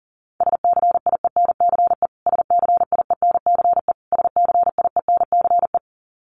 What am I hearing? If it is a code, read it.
Morse code, heard as a keyed tone, about 40 wpm, 700 Hz. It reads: SCIENCE SCIENCE SCIENCE
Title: Top 500 Words Repeated 40wpm